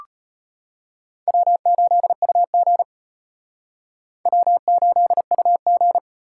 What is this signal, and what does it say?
Morse code, heard as a keyed tone, about 31 wpm, 700 Hz. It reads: W8UG W8UG
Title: Speed Racing - USA Call Signs 25wpm